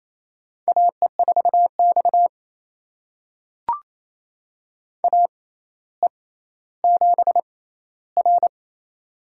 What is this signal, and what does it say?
Morse code, heard as a keyed tone, about 27 wpm, spaced as for 14 wpm, 700 Hz. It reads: AE4X AE7R
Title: USA Call Signs - 4 Characters - 6x Character Spacing - 28wpm